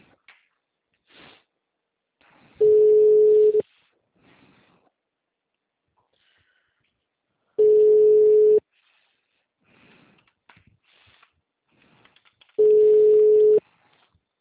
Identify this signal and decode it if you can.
Ringback tone (Europe, 425 Hz).
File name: out- -2016- - - .5820